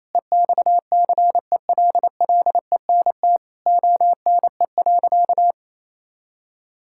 Morse code, heard as a keyed tone, about 27 wpm, 700 Hz. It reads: EXCELLENT ONE.
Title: Sentences from Top 1000 Words - Review Part 6 - 28wpm